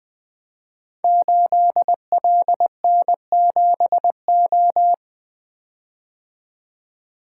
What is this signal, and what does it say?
Morse code, heard as a keyed tone, about 20 wpm, 700 Hz. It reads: 8LN7O
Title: Code Groups - 5 Letter:Numbers - 20wpm